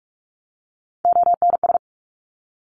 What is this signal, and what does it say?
Morse code, heard as a keyed tone, about 46 wpm, 690 Hz. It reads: ONS
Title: Top Three Letter Combinations in Words 45wpm